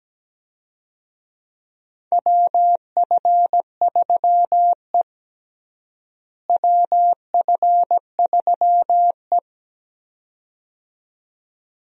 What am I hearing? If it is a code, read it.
Morse code, heard as a keyed tone, about 17 wpm, 700 Hz. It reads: WF3E WF3E
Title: USA Call Signs - 4 Characters - Repeated 2x - 17wpm